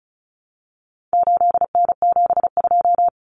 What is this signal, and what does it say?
Morse code, heard as a keyed tone, about 34 wpm, 700 Hz. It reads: 8N72
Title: Common Cut Numbers - 4-Digit 35wpm